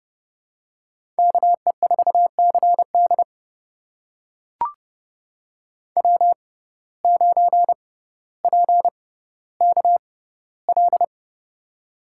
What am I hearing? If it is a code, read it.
Morse code, heard as a keyed tone, about 30 wpm, spaced as for 15 wpm, 700 Hz. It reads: KE4CD W9PKL
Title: USA Call Signs - 5 Characters - 6x Character Spacing - 30wpm